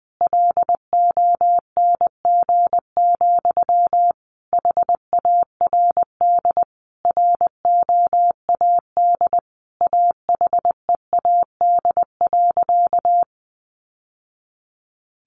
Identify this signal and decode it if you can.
Morse code, heard as a keyed tone, about 20 wpm, 700 Hz. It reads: LONG, HARD ROAD AHEAD.
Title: Sentences from Top 1100 Words 20wpm